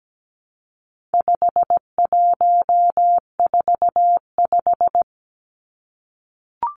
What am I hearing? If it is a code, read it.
Morse code, heard as a keyed tone, about 17 wpm, 700 Hz. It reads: 5145
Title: Numbers - 4-Digit 17wpm